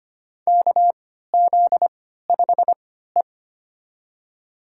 Morse code, heard as a keyed tone, about 25 wpm, spaced as for 14 wpm, 700 Hz. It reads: KZ5E
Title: USA Call Signs - 3x Character Spacing - 25wpm